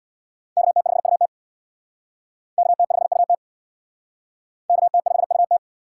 Morse code, heard as a keyed tone, about 59 wpm, 700 Hz. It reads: KT5UT KT5UT KT5UT
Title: Speed Racing - USA 2x2 Call Signs 50wpm